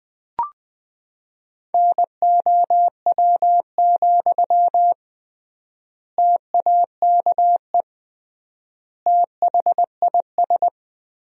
Morse code, heard as a keyed tone, about 20 wpm, 700 Hz. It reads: NOW, TAKE THIS
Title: 3x Word-Spacing - Sentences from Top 400 Words - 20wpm